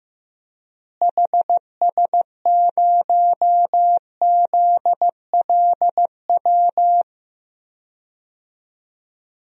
Morse code, heard as a keyed tone, about 15 wpm, 700 Hz. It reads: HS0ZLW
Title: DXCC-100 Call Signs - 6 Characters - Repeated 2x - 15wpm